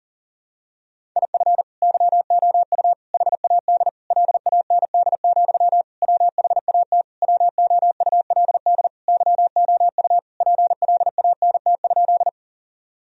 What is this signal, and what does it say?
Morse code, heard as a keyed tone, about 40 wpm, 700 Hz. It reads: IF YOU HAD LAND, WHAT WOULD YOU PLANT?
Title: Sentences from Top 1000 Words - Review Part 1 - 40wpm